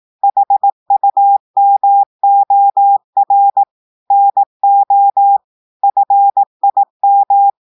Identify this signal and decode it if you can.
Morse code, heard as a keyed tone, about 18 wpm, 800 Hz. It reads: HUMOR NO FIM
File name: audios / portugais / portugais texte en clair 2 vitesse 18 mots minutes